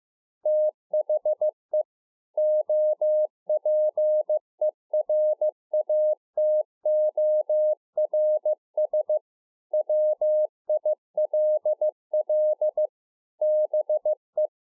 Morse code, heard as a keyed tone, about 15 wpm, 610 Hz. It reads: THE OPERATORS WILL BE